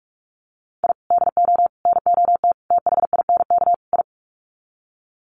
Morse code, heard as a keyed tone, about 44 wpm, 700 Hz. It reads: I DO NOT THINK I
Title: Sets of 5 Words 45wpm